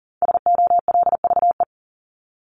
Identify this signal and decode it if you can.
Morse code, heard as a keyed tone, about 40 wpm, 700 Hz. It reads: SOLVE
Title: Top 751-1000 Words 40wpm